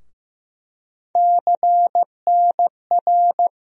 Morse code, heard as a keyed tone, about 15 wpm, 700 Hz. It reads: CNR
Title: USA Call Sign Postfixes - 3 Letters 15wpm